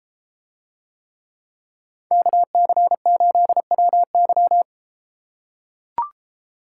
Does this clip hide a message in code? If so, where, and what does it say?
Yes: Morse code from 2.1 to 4.6 s, heard as a keyed tone, about 32 wpm, 700 Hz. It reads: KC8WY